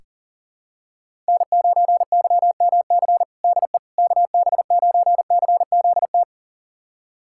Morse code, heard as a keyed tone, about 40 wpm, 700 Hz. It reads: N9YMC DE KB9CZT
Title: USA Callsign DE Callsign 40wpm